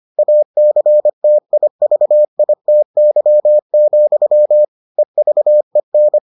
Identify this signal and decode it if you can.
Morse code, heard as a keyed tone, about 25 wpm, 600 Hz. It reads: ACTIVITY, EVEN